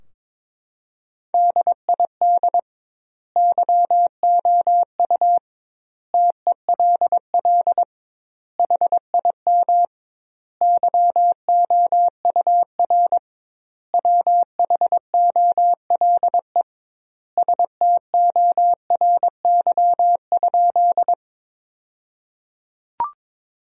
Morse code, heard as a keyed tone, about 22 wpm, 700 Hz. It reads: DID YOU TELL HIM YOUR WHOLE STORY?
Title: 2x Word-Spacing - Sentences from Top 400 Words - 22wpm